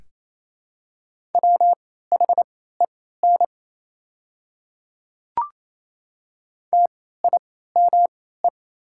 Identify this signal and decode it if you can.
Morse code, heard as a keyed tone, about 28 wpm, spaced as for 16 wpm, 700 Hz. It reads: WHEN TIME
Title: 3x Character Spacing - Top 100 Words - 28wpm